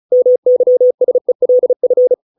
Morse code, heard as a keyed tone, about 35 wpm, 500 Hz. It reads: MYSELF